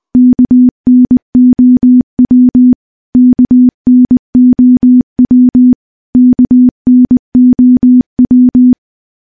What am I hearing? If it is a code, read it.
Morse code, heard as a keyed tone, about 21 wpm, 250 Hz. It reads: KNOW KNOW KNOW
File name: morse / morse-training-20WPM